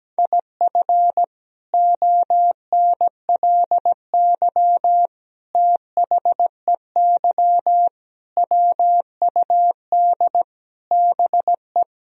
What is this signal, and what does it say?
Morse code, heard as a keyed tone, about 17 wpm, 700 Hz. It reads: IF ONLY THEY WUD BE
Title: Sentences from Top 500 Words and Abbreviations 17wpm